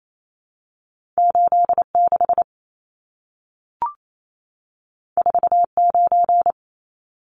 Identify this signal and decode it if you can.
Morse code, heard as a keyed tone, about 28 wpm, 700 Hz. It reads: 86 49
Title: Numbers - 2-Digit 28wpm